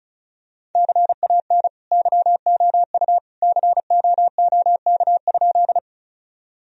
Morse code, heard as a keyed tone, about 34 wpm, 700 Hz. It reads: CAN YOU COOK?
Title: Sentences from Top 1000 Words - Review Part 4 - 35wpm